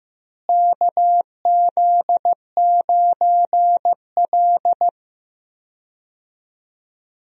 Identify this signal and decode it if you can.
Morse code, heard as a keyed tone, about 15 wpm, 700 Hz. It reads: KZ9L
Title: USA Call Signs - 4 Characters - Repeated 3x - 15wpm